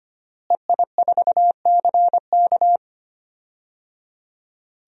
Morse code, heard as a keyed tone, about 25 wpm, 700 Hz. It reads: EI4CK
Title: DXCC-100 Call Signs - 5 Characters - Repeated 3x - 25wpm